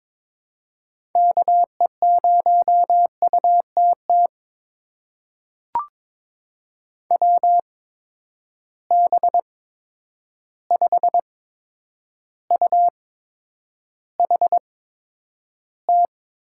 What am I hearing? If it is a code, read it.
Morse code, heard as a keyed tone, about 22 wpm, spaced as for 9 wpm, 700 Hz. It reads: KE0UTT WB5UHT
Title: USA Call Signs - 6 Characters - 8x Character Spacing - 22wpm